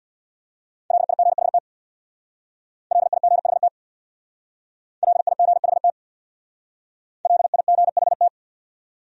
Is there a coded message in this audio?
Yes: Morse code from 0.9 to 8.3 s, heard as a keyed tone, about 64 wpm, 700 Hz. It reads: LIGHT LIGHT LIGHT LIGHT